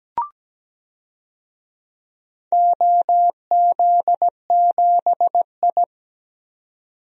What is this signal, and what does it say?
Morse code, heard as a keyed tone, about 17 wpm, 700 Hz. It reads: OZ7I
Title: DXCC-100 Call Signs - 4 Characters - 17wpm